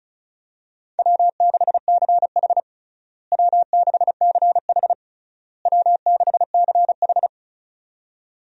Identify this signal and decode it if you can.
Morse code, heard as a keyed tone, about 36 wpm, 700 Hz. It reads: W6CH W6CH W6CH
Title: USA Call Signs - 4 Characters - Repeated 3x - 35wpm